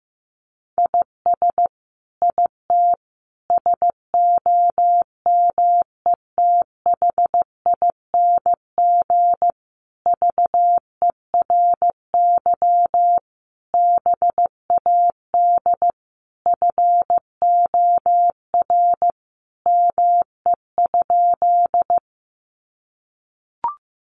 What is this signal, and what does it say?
Morse code, heard as a keyed tone, about 15 wpm, 700 Hz. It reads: IS IT SOMETHING VERY BAD FOR ME?
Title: Sentences from Top 700 Words 15wpm